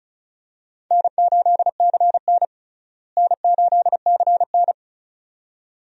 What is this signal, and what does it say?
Morse code, heard as a keyed tone, about 36 wpm, 700 Hz. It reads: N8CN N8CN
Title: USA Call Signs - 4 Characters - Repeated 2x - 35wpm